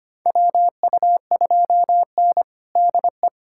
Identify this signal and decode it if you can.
Morse code, heard as a keyed tone, about 25 wpm, 700 Hz. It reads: WU2N DE
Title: USA Callsign DE Callsign - 4 Characters - 25wpm